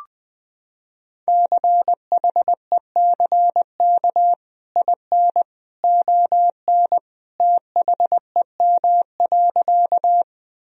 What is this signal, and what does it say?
Morse code, heard as a keyed tone, about 20 wpm, 700 Hz. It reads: CHECK IN ON THEM.
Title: Sentences from Top 1000 Words - Review Part 3 - 20wpm